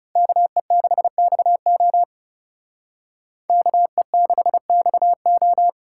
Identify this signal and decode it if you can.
Morse code, heard as a keyed tone, about 38 wpm, 700 Hz. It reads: KE6XO KE6XO